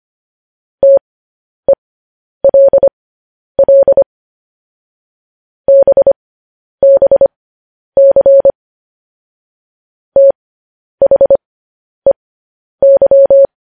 Morse code, heard as a keyed tone, about 25 wpm, spaced as for 10 wpm, 550 Hz. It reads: TELL BBC THEY
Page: News Headlines in Morse Code at 10 WPM